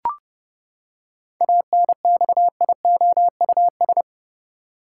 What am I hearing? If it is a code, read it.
Morse code, heard as a keyed tone, about 32 wpm, 700 Hz. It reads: ANXIOUS
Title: Seven Letter Words - 30wpm